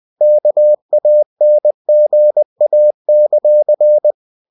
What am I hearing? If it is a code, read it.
Morse code, heard as a keyed tone, about 20 wpm, 600 Hz. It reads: KANGA;
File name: cw-books / winnie-the-pooh / winnie-20wpm-page-0093